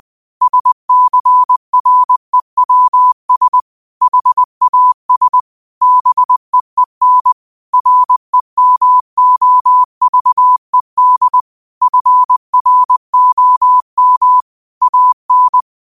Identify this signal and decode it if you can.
Morse code, heard as a keyed tone, about 20 wpm, 1000 Hz. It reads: SCREWS HAS BEEN REMOVED FROM AN
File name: qotdfiles / 20wpm